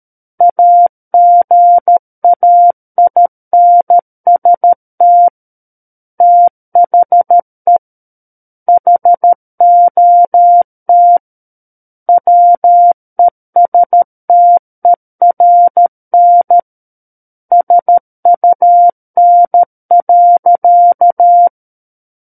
Morse code, heard as a keyed tone, about 13 wpm, 700 Hz. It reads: AGAINST THE HOT WESTERN SUN.
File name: Book One / 04-Chapter 4 - 13 WPM